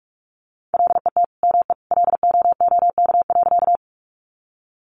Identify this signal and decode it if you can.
Morse code, heard as a keyed tone, about 44 wpm, 700 Hz. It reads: LET ME LOOK.
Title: Sentences from Top 300 Words 45wpm